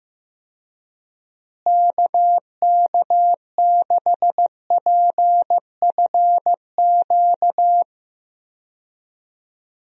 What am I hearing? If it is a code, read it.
Morse code, heard as a keyed tone, about 15 wpm, 700 Hz. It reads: KK6PFQ
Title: USA Call Signs - 6 Characters - Repeated 3x - 15wpm